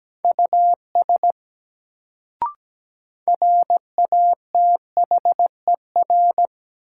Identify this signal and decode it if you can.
Morse code, heard as a keyed tone, about 17 wpm, 700 Hz. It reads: US RATHER
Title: Sets of 2 Words - New Word Drill - 17wpm